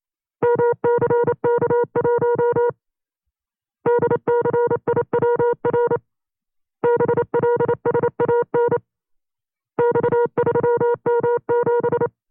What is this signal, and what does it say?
Morse code, heard as a keyed tone, about 27 wpm, 480 Hz. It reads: MCK1 DCIWR BLSAN X3M7
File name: CW-practice/25WPM